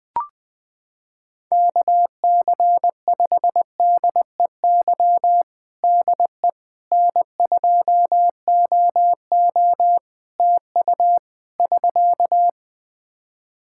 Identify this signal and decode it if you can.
Morse code, heard as a keyed tone, about 20 wpm, 700 Hz. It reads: KC5DEY DE N2OO TU <SK>
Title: K1USN SST QSO Practice 20wpm